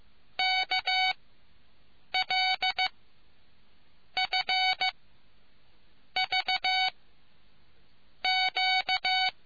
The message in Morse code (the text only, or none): KLFVQ